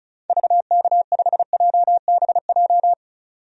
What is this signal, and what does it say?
Morse code, heard as a keyed tone, about 36 wpm, 700 Hz. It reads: VK5JBJ
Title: DXCC-100 Call Signs - 6 Characters - 35wpm